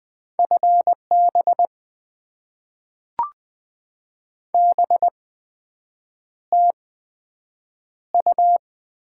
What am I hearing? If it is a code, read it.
Morse code, heard as a keyed tone, about 20 wpm, spaced as for 7 wpm, 700 Hz. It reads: FB BTU
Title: Top 20 QSO Elements - 8x Character Spacing - 20wpm